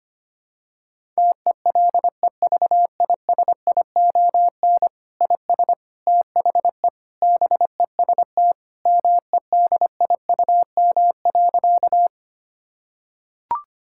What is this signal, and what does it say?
Morse code, heard as a keyed tone, about 25 wpm, 700 Hz. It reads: TELEVISION IS THE BEST MEDIUM.
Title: Sentences from Top 1000 Words - Review Part 6 - 25wpm